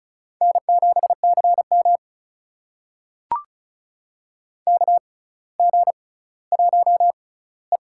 Morse code, heard as a keyed tone, about 36 wpm, spaced as for 18 wpm, 700 Hz. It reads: N7CM KG1E
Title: USA Call Signs - 6x Character Spacing - 35wpm